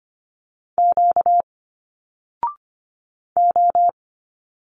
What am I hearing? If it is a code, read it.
Morse code, heard as a keyed tone, about 25 wpm, 700 Hz. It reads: Q O
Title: Single Letters 25wpm